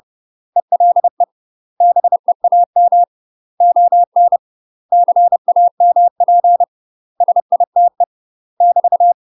Morse code, heard as a keyed tone, about 30 wpm, 700 Hz. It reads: ELE BEAM ON CAMP SITE <BT>
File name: UK MASTER QSOs TRAINING / 30WPM / UK MASTER QSOs 0022